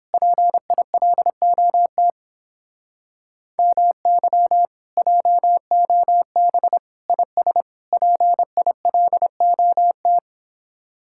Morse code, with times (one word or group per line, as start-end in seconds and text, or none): 0.1-2.1 PILOT
3.6-4.7 MY
5.0-6.8 JOB
7.1-7.6 IS
7.9-10.2 PILOT